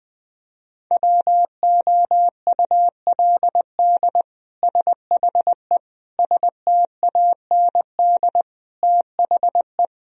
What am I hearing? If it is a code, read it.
Morse code, heard as a keyed tone, about 20 wpm, 700 Hz. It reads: WOULD SHE STAND THE
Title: Sentences from Top 1000 Words - Review Part 3 - 20wpm